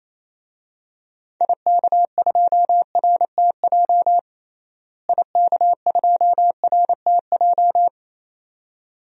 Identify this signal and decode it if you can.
Morse code, heard as a keyed tone, about 27 wpm, 700 Hz. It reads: IK2RTJ IK2RTJ
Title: DXCC-100 Call Signs - 6 Characters - Repeated 2x - 28wpm